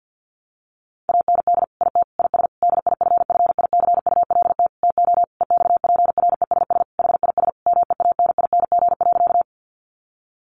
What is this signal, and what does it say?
Morse code, heard as a keyed tone, about 50 wpm, 700 Hz. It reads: AND IT IS DIFFICULT TO EXPRESS HIS MEANING.
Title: Sentences from Top 1100 Words 50wpm